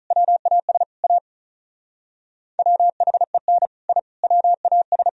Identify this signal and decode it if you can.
Morse code, heard as a keyed tone, about 43 wpm, 700 Hz. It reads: WAS A WHEN I WAS
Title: Speed-Racing - Sets of 4 Words 35wpm